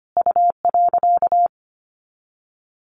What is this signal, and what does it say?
Morse code, heard as a keyed tone, about 25 wpm, 700 Hz. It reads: U.
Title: Sets of 3 Words 25wpm